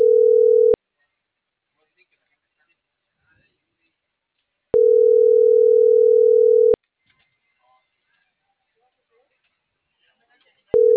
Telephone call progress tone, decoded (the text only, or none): ringback tone